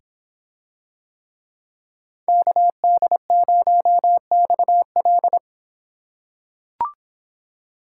Morse code, heard as a keyed tone, about 26 wpm, 700 Hz. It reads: KD0XL